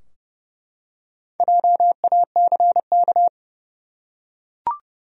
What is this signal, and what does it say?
Morse code, heard as a keyed tone, about 31 wpm, 700 Hz. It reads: JACK
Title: USA Common Names - Max 5 Letters - 30wpm